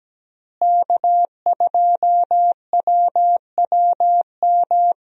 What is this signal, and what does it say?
Morse code, heard as a keyed tone, about 17 wpm, 700 Hz. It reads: K2WWM
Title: USA Call Signs - 5 Characters - Repeated 2x - 17wpm